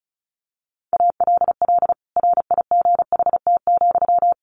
Morse code, heard as a keyed tone, about 34 wpm, 700 Hz. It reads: ALL RIGHT,
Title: Sentences from Top 600 Words 35wpm